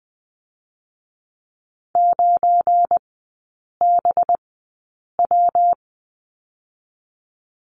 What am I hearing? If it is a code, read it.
Morse code, heard as a keyed tone, about 20 wpm, spaced as for 8 wpm, 700 Hz. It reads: 9BW
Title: 218 Lesson 35e - Number 0 - Three Characters 20wpm